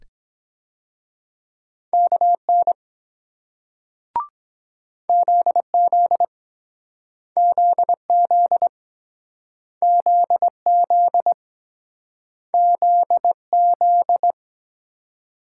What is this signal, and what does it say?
Morse code, heard as a keyed tone, about 23 wpm, 700 Hz. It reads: KN ZZ ZZ ZZ ZZ